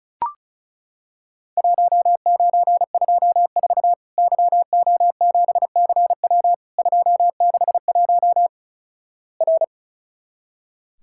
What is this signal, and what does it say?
Morse code, heard as a keyed tone, about 36 wpm, 700 Hz. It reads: 1924 YO7CW 261 R
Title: Worked All Europe DX Contest - QTC Exchange - 35wpm